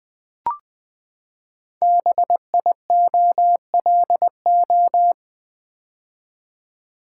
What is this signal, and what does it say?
Morse code, heard as a keyed tone, about 20 wpm, 700 Hz. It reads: BIOLO
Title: Build A Word - 5 Letters At A Time - 10-Letter Words - 20wpm